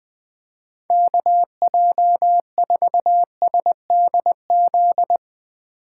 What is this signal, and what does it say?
Morse code, heard as a keyed tone, about 20 wpm, 700 Hz. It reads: KJ4SDZ